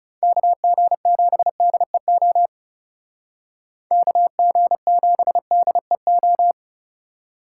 Morse code, heard as a keyed tone, about 37 wpm, 700 Hz. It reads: KG7DEO KG7DEO
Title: Speed Racing - USA 2x3 Call Signs 30wpm